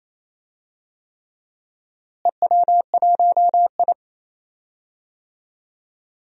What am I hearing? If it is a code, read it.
Morse code, heard as a keyed tone, about 28 wpm, 700 Hz. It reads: EW1I
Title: DXCC-100 Call Signs - 4 Characters - Repeated 3x - 28wpm